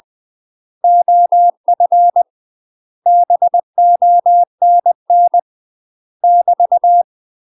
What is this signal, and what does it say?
Morse code, heard as a keyed tone, about 20 wpm, 700 Hz. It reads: OF BONN <BT>
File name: UK MASTER QSOs TRAINING / 20WPM / UK MASTER QSOs 0068